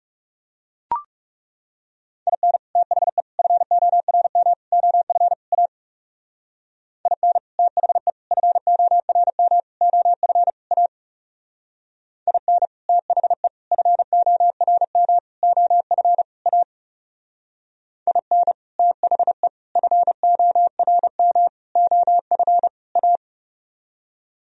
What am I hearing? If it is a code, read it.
Morse code, heard as a keyed tone, about 37 wpm, 700 Hz. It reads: IN THE FORM OF A IN THE FORM OF A IN THE FORM OF A IN THE FORM OF A
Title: Speed-Racing - Sets of 5 Words 30wpm